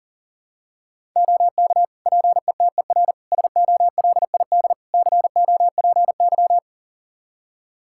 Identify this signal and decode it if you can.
Morse code, heard as a keyed tone, about 40 wpm, 700 Hz. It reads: OK PETER SOLID COPY
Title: K1USN SST QSO Practice 40wpm